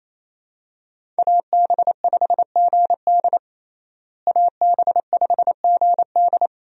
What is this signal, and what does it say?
Morse code, heard as a keyed tone, about 28 wpm, 700 Hz. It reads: AB5GD AB5GD
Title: USA Call Signs - 5 Characters - Repeated 2x - 28wpm